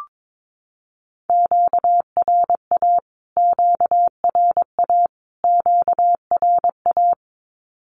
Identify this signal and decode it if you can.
Morse code, heard as a keyed tone, about 22 wpm, 700 Hz. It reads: QRA QRA QRA
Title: Q-Codes Repeated 3x 22wpm